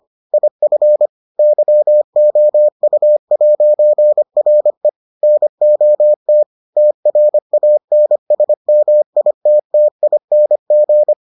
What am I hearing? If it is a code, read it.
Morse code, heard as a keyed tone, about 25 wpm, 600 Hz. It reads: IF YOU'RE NOT TRANSMITTING